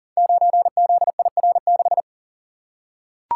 Morse code, heard as a keyed tone, about 42 wpm, 700 Hz. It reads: 9ZIR6
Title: Code Groups - 5 Letter:Numbers - 40wpm